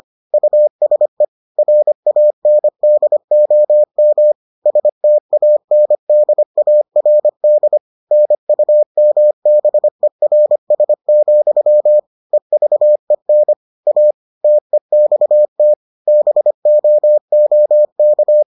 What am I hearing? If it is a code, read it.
Morse code, heard as a keyed tone, about 25 wpm, 600 Hz. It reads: USE RANDOM STANDARD NUMBERS, EVEN A TEXT BOOK